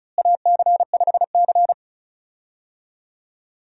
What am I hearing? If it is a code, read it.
Morse code, heard as a keyed tone, about 36 wpm, 700 Hz. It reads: AC5C
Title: USA Call Signs - 4 Characters - Repeated 2x - 35wpm